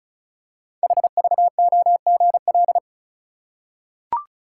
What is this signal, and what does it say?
Morse code, heard as a keyed tone, about 36 wpm, 690 Hz. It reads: HVOGL
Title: Code Groups - 5 Letter:Numbers - 35wpm